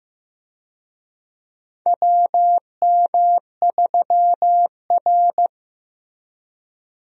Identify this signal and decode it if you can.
Morse code, heard as a keyed tone, about 15 wpm, 700 Hz. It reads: WM3R
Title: USA Call Signs - 4 Characters - Repeated 3x - 15wpm